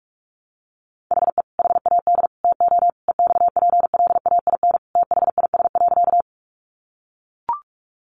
Morse code, heard as a keyed tone, about 44 wpm, 700 Hz. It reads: HE HAD TO EXPLAIN THIS.
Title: Sentences from Top 1100 Words 45wpm